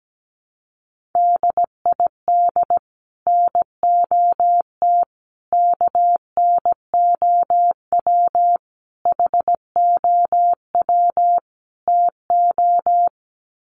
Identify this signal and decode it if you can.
Morse code, heard as a keyed tone, about 17 wpm, 700 Hz. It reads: DID NOT KNOW HOW TO
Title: Sets of 5 Words 17wpm